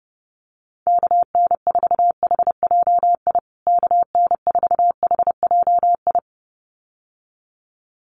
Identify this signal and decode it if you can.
Morse code, heard as a keyed tone, about 30 wpm, 700 Hz. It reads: KN4HJI KN4HJI
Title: USA Call Signs Repeated 2x 30wpm